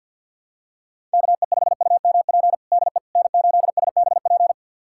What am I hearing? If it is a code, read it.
Morse code, heard as a keyed tone, about 47 wpm, 700 Hz. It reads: KE5UMP DE N8SBP
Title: USA Callsign DE Callsign 50wpm